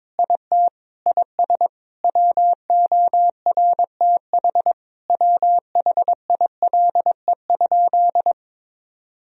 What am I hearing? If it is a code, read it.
Morse code, heard as a keyed tone, about 22 wpm, 700 Hz. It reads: IT IS WORTH WHILE?
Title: Sentences from Top 1000 Words - Review Part 1 - 22wpm